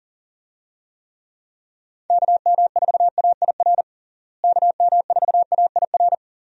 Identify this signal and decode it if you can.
Morse code, heard as a keyed tone, about 40 wpm, 700 Hz. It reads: KM4AIR KM4AIR
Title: USA Call Signs - 6 Characters - Repeated 2x - 40wpm